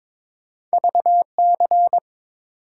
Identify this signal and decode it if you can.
Morse code, heard as a keyed tone, about 30 wpm, 700 Hz. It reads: VC